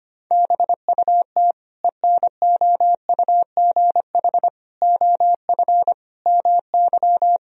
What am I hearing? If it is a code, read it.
Morse code, heard as a keyed tone, about 25 wpm, 700 Hz. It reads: BUT ENOUGH OF MY
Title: Sentences from Top 1000 Words - Review Part 2 - 25wpm